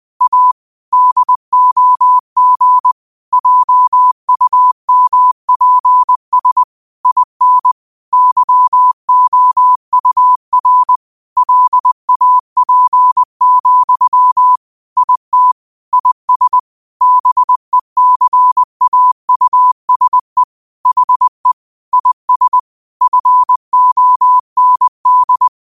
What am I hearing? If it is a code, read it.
Morse code, heard as a keyed tone, about 20 wpm, 1000 Hz. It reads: A DOG JUMPS IN YOUR LAP, IT IS BECAUSE HE IS FOND